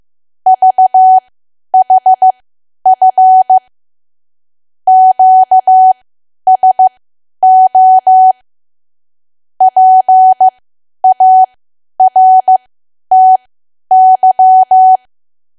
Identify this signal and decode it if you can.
Morse code, heard as a keyed tone, about 15 wpm, spaced as for 10 wpm, 750 Hz. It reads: VHF QSO PARTY
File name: Archive / 10 WPM / 10WPM